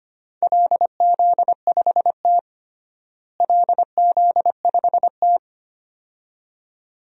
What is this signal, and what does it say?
Morse code, heard as a keyed tone, about 25 wpm, 700 Hz. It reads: LZ5T LZ5T
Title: DXCC-100 Call Signs - 4 Characters - Repeated 2x - 25wpm